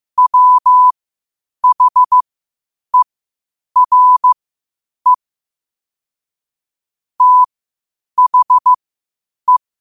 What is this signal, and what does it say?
Morse code, heard as a keyed tone, about 15 wpm, spaced as for 8 wpm, 1000 Hz. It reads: WHERE THE
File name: qotdfiles / 8wpm